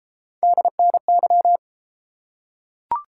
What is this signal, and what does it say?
Morse code, heard as a keyed tone, about 33 wpm, 700 Hz. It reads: DNY